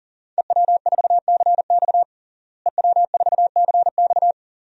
Morse code, heard as a keyed tone, about 40 wpm, 700 Hz. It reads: EW4CX EW4CX
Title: DXCC-100 Call Signs - 5 Characters - Repeated 2x - 40wpm